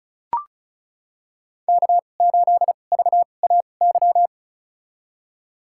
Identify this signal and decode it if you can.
Morse code, heard as a keyed tone, about 36 wpm, spaced as for 25 wpm, 700 Hz. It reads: K8VAY
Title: USA Call Signs - 5 Characters - 2x Character Spacing - 35wpm